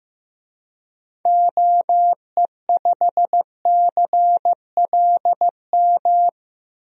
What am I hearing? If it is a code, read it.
Morse code, heard as a keyed tone, about 15 wpm, 700 Hz. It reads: OE5CLM
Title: DXCC-100 Call Signs - 6 Characters - 15wpm